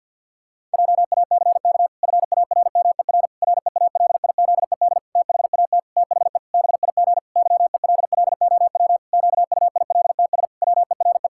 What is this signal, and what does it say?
Morse code, heard as a keyed tone, about 52 wpm, 700 Hz. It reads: JACK FARMER REALIZED THAT THE BIG YELLOW QUILTS WERE
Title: Pangram Challenge v2 - 50wpm